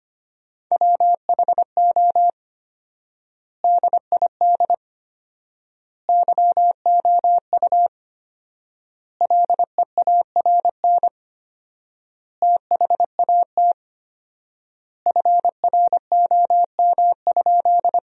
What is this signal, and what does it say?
Morse code, heard as a keyed tone, about 25 wpm, 700 Hz. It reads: WHO DID YOU LEARN THAT FROM?
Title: 4x Word-Spacing - Sentences from Top 300 Words - 25wpm